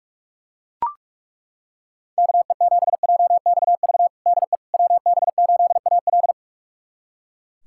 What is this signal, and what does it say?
Morse code, heard as a keyed tone, about 43 wpm, 700 Hz. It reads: KE7JXV DE WB8AL
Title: USA Callsign DE Callsign 45wpm